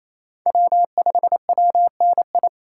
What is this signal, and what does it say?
Morse code, heard as a keyed tone, about 28 wpm, 700 Hz. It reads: W5WNI
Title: USA Call Signs - 5 Characters - 2x Character Spacing - 28wpm